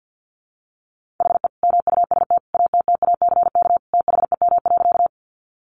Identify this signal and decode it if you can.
Morse code, heard as a keyed tone, about 50 wpm, 700 Hz. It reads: HE MUST ATTACK THEM.
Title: Sentences from Top 1500 Words 50wpm